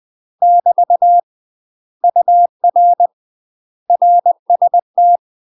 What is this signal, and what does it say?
Morse code, heard as a keyed tone, about 20 wpm, 700 Hz. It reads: <BT> UR RST
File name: UK MASTER QSOs TRAINING/20WPM